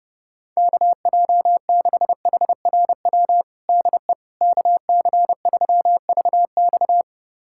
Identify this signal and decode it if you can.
Morse code, heard as a keyed tone, about 31 wpm, 700 Hz. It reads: KJ6HRW DE KC3VX
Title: USA Callsign DE Callsign 30wpm